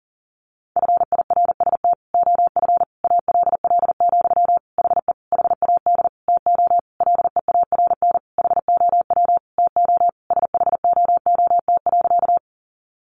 Morse code, heard as a keyed tone, about 40 wpm, 700 Hz. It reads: FIRST OF ALL, HE HAD TO LEARN HOW TO SHOOT.
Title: Sentences from Top 1200 Words 40wpm